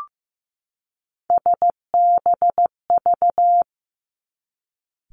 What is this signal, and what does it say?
Morse code, heard as a keyed tone, about 15 wpm, 700 Hz. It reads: SBV
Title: Three Challenging Letter-Number Combinations 15wpm